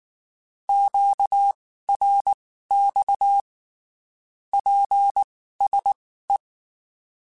Morse code, heard as a keyed tone, about 19 wpm, spaced as for 14 wpm, 800 Hz. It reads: QRX PSE
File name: (spaced) / QSOs Spaced-mp3s / kc7haa @18 sp